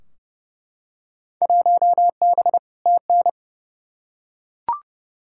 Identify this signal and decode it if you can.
Morse code, heard as a keyed tone, about 32 wpm, 700 Hz. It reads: 1B TN
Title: Field Day Exchanges - 30wpm